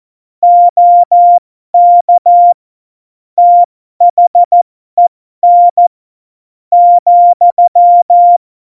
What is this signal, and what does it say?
Morse code, heard as a keyed tone, about 14 wpm, spaced as for 12 wpm, 700 Hz. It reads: OK THEN ,
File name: Morsekurs / Witze